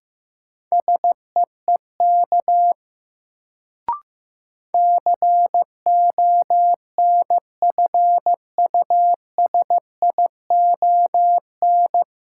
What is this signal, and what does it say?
Morse code, heard as a keyed tone, about 15 wpm, 700 Hz. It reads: SEEK CONFUSION
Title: Sentences from Top 1600 Words - New Word Drill Alt - 15wpm